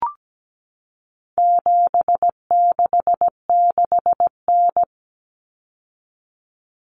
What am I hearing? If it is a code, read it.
Morse code, heard as a keyed tone, about 17 wpm, 700 Hz. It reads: 766N
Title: Common Cut Numbers - 4-Digit 17wpm